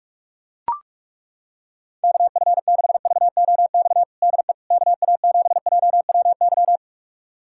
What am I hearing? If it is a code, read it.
Morse code, heard as a keyed tone, about 41 wpm, 700 Hz. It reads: KF6VOX DE KA7JWY
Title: USA Callsign DE Callsign - 6 Characters - 45wpm